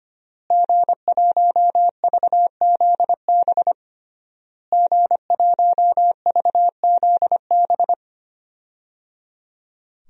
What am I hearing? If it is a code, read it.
Morse code, heard as a keyed tone, about 25 wpm, 700 Hz. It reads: G1VZB G1VZB
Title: DXCC-100 Call Signs - 5 Characters - Repeated 3x - 25wpm